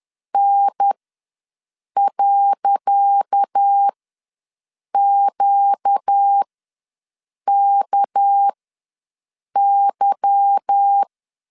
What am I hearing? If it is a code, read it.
Morse code, heard as a keyed tone, about 10 wpm, spaced as for 6 wpm, 800 Hz. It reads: N.QKY
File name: LEZIONI / 10 SIGNS / 10C DOT ws 10wpm 4213